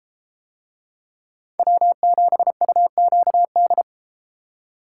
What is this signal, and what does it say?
Morse code, heard as a keyed tone, about 33 wpm, 700 Hz. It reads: W7UQD